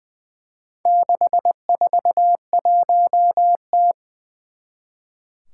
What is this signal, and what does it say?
Morse code, heard as a keyed tone, about 20 wpm, 700 Hz. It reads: 641T
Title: Four Challenging Letter-Number Combinations 20wpm